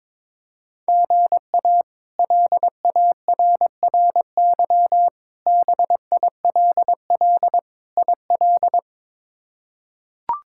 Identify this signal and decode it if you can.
Morse code, heard as a keyed tone, about 21 wpm, 700 Hz. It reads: GA LARRY BILL IL
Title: K1USN SST QSO Practice - 22wpm